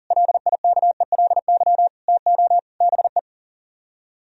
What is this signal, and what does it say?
Morse code, heard as a keyed tone, about 49 wpm, 700 Hz. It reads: LIKELY TO BE
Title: Speed-Racing - Sets of 5 Words 40wpm